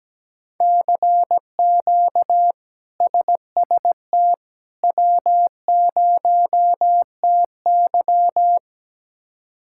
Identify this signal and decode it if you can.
Morse code, heard as a keyed tone, about 17 wpm, 700 Hz. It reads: CQ SST W0TY
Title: K1USN SST QSO Practice - 17wpm